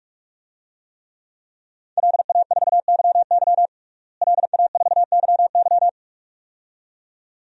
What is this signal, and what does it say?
Morse code, heard as a keyed tone, about 46 wpm, 700 Hz. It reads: LA4YY LA4YY
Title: DXCC-100 Call Signs - 5 Characters - Repeated 2x - 45wpm